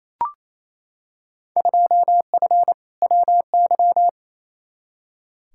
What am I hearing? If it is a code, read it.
Morse code, heard as a keyed tone, about 28 wpm, 700 Hz. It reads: 2F WY
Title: Field Day Exchanges Encore - 28wpm